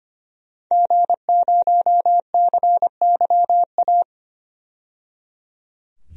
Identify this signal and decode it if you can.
Morse code, heard as a keyed tone, about 25 wpm, 700 Hz. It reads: G0CYA
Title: DXCC-100 Call Signs - 5 Characters - 25wpm